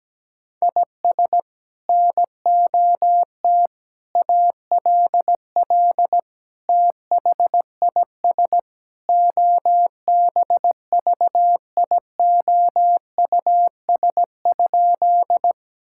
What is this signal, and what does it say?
Morse code, heard as a keyed tone, about 17 wpm, 700 Hz. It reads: IS NOT ALL THIS OBVIOUS?
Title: Sentences from Top 1600 Words 17wpm